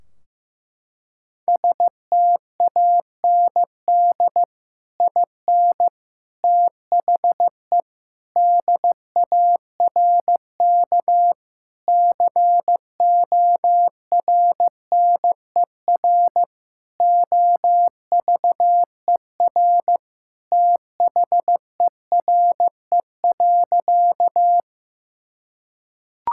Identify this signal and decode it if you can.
Morse code, heard as a keyed tone, about 15 wpm, 700 Hz. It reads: STAND IN THE DARK CORNER OVER THERE.
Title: Sentences from Top 1600 Words 15wpm